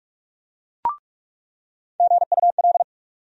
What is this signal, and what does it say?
Morse code, heard as a keyed tone, about 46 wpm, 700 Hz. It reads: GUL
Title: USA Call Sign Postfixes - 3 Letters 45wpm